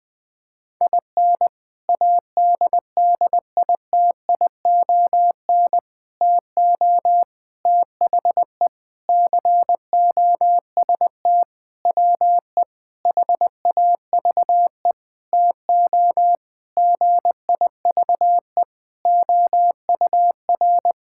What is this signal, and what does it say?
Morse code, heard as a keyed tone, about 21 wpm, 700 Hz. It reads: IN ADDITION TO THE COST WE HAVE TO GIVE OUR
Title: Sentences with 4-Word Phrases - 20wpm